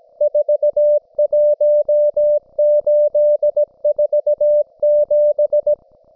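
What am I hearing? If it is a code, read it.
Morse code, heard as a keyed tone, about 17 wpm, 590 Hz. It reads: 41847
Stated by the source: Russian military Morse station, M32, RDL, on 5268 kHz.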